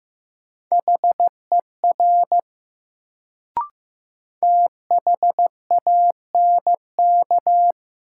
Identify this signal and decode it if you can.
Morse code, heard as a keyed tone, about 15 wpm, 700 Hz. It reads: HER THANK
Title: Sets of 2 Words - New Word Drill - 15wpm